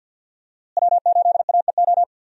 Morse code, heard as a keyed tone, about 47 wpm, 700 Hz. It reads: W8AEO
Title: USA Call Signs - 5 Characters - Repeated 2x - 50wpm